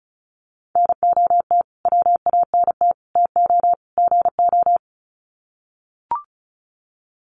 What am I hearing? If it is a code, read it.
Morse code, heard as a keyed tone, about 34 wpm, 700 Hz. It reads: NOT WANT TO GO
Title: Sets of 4 Words 35wpm